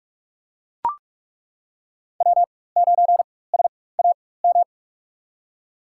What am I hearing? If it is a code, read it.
Morse code, heard as a keyed tone, about 41 wpm, spaced as for 20 wpm, 700 Hz. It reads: W9SAM
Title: USA Call Signs - 5 Characters - 4x Character Spacing - 45wpm